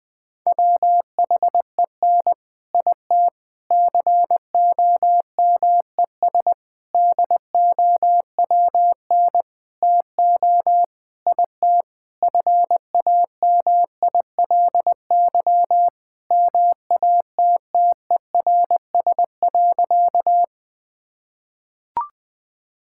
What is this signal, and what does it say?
Morse code, heard as a keyed tone, about 21 wpm, 700 Hz. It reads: WHEN IT COMES DOWN TO IT FAMILY MATTERS.
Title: Sentences with 6-Word Phrases - Spoken Beginning - 20wpm